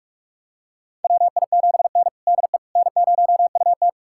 Morse code, heard as a keyed tone, about 46 wpm, 700 Hz. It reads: WI7N DE N0UT
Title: USA Callsign DE Callsign - 4 Characters - 45wpm